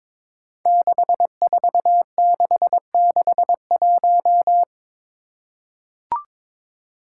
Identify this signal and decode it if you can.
Morse code, heard as a keyed tone, about 21 wpm, 700 Hz. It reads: 64661
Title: Five Challenging Letter-Number Combinations 22wpm